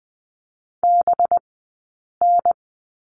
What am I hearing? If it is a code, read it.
Morse code, heard as a keyed tone, about 20 wpm, 700 Hz. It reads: B N
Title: 183 Lesson 30d - Letter b - Two Characters 20wpm